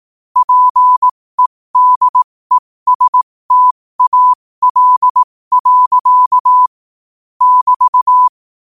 Morse code, heard as a keyed tone, about 18 wpm, spaced as for 15 wpm, 1000 Hz. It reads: PEDESTAL. <BT>
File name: qotdfiles / 15wpm